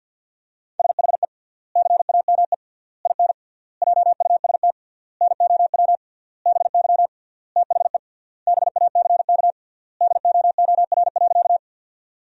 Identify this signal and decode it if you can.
Morse code, heard as a keyed tone, about 51 wpm, 700 Hz. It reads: SHE CAME IN JUST NOW BY THE BACK DOOR.
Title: 3x Word-Spacing - Sentences from Top 300 Words - 50wpm